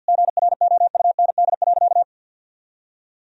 Morse code, heard as a keyed tone, about 51 wpm, 700 Hz. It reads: GROUND.
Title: Sentences from Top 1000 Words - Review Part 2 - 50wpm